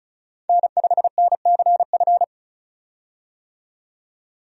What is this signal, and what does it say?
Morse code, heard as a keyed tone, about 36 wpm, 700 Hz. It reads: N5NCF
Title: USA Call Signs - 5 Characters - 35wpm